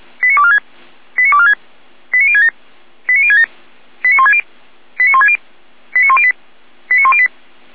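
Five-tone selective call (ZVEI), heard as quick runs of five tones, tones of about 70 ms. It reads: 89246 89246 89067 89067 89160 89160 89108 89108
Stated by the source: ZVEI five-tone call received on SDR.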